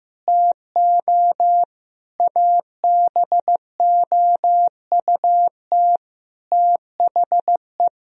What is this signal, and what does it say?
Morse code, heard as a keyed tone, about 15 wpm, 700 Hz. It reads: TO ABOUT THE
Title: Sentences from Top 1000 Words - Review Part 4 - 15wpm